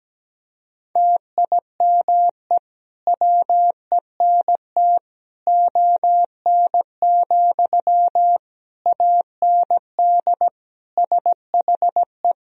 Morse code, heard as a keyed tone, about 17 wpm, 700 Hz. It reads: TIME WENT ON, AND SHE
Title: Sentences from Top 1000 Words - Review Part 3 - 17wpm